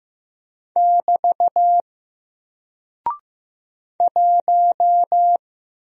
Morse code, heard as a keyed tone, about 15 wpm, 700 Hz. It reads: <BT> 1
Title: Single Character (Letter:Number:Punctuation:Prosign) 15wpm